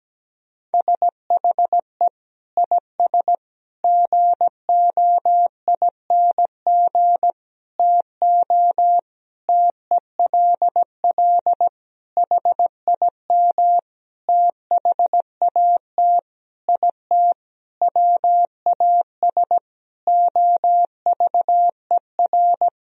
Morse code, heard as a keyed tone, about 17 wpm, 700 Hz. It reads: SHE IS GOING TO TELL HIM THAT IT WAS OVER
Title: Sentences from Top 1000 Words - Review Part 1 - 17wpm